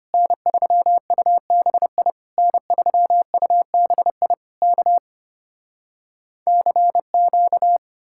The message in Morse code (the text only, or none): N3UBI N3UBI K CQ